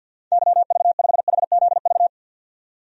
Morse code, heard as a keyed tone, about 52 wpm, 700 Hz. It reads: YV5HZV
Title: DXCC-100 Call Signs - 6 Characters - 50wpm